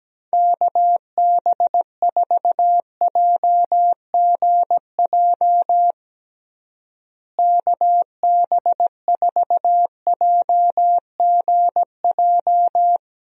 Morse code, heard as a keyed tone, about 17 wpm, 700 Hz. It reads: KB4JGJ KB4JGJ
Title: USA Call Signs - 6 Characters - Repeated 3x - 17wpm